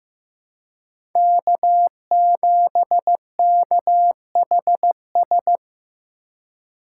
Morse code, heard as a keyed tone, about 15 wpm, 700 Hz. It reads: K7KHS
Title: USA Call Signs - 5 Characters - 2x Character Spacing - 15wpm